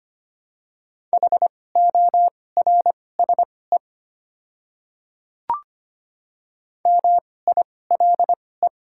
Morse code, heard as a keyed tone, about 25 wpm, spaced as for 18 wpm, 690 Hz. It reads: HORSE MILE
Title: 2x Character Spacing - Top 300 Words - 25wpm